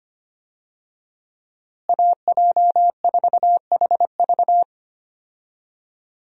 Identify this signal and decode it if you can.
Morse code, heard as a keyed tone, about 25 wpm, 700 Hz. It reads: AJ4HV
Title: USA Call Signs - 5 Characters - Repeated 3x - 25wpm